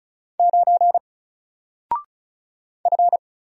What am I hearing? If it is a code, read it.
Morse code, heard as a keyed tone, about 35 wpm, 700 Hz. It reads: 9 F
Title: Single Character (Letter:Number:Punctuation:Prosign) 35wpm